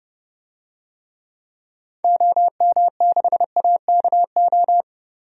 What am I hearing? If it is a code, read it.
Morse code, heard as a keyed tone, about 30 wpm, 700 Hz. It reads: OM6AKO
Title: DXCC-100 Call Signs - 6 Characters - 30wpm